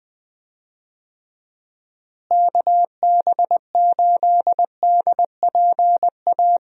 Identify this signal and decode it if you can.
Morse code, heard as a keyed tone, about 20 wpm, 700 Hz. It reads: KB8DPA
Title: USA Call Signs - 6 Characters - Repeated 2x - 20wpm